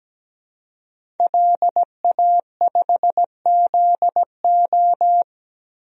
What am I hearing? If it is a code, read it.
Morse code, heard as a keyed tone, about 17 wpm, 700 Hz. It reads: LA5ZO
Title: DXCC-100 Call Signs - 5 Characters - 17wpm